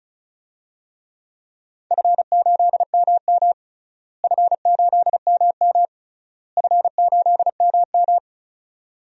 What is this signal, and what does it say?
Morse code, heard as a keyed tone, about 36 wpm, 700 Hz. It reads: F8MM F8MM F8MM
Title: DXCC-100 Call Signs - 4 Characters - Repeated 3x - 35wpm